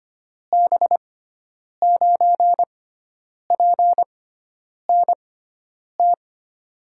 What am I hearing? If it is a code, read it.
Morse code, heard as a keyed tone, about 25 wpm, 700 Hz. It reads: B 9 P N T
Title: USA Call Signs - 6 Characters - 6x Character Spacing - 25wpm